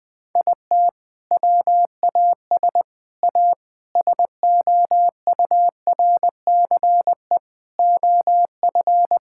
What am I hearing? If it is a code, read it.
Morse code, heard as a keyed tone, about 20 wpm, 700 Hz. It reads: IT WAS A SOURCE OF
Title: Sentences from Top 1000 Words - Review Part 5 - 20wpm